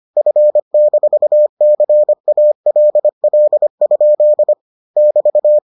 Morse code, heard as a keyed tone, about 25 wpm, 600 Hz. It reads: F-CALL? <BT>